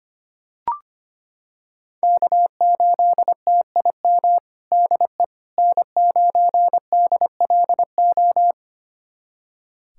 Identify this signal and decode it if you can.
Morse code, heard as a keyed tone, about 25 wpm, 700 Hz. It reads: K8TIM DE N9DLO
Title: USA Callsign DE Callsign - 5 Characters - 25wpm